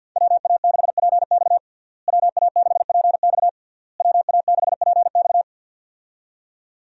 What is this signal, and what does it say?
Morse code, heard as a keyed tone, about 48 wpm, 700 Hz. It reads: WA6PX WA6PX WA6PX
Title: USA Call Signs - 5 Characters - Repeated 3x - 50wpm